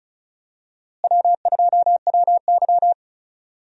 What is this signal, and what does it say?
Morse code, heard as a keyed tone, about 36 wpm, 700 Hz. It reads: W2WY
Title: USA Call Signs - 4 Characters - Repeated 3x - 35wpm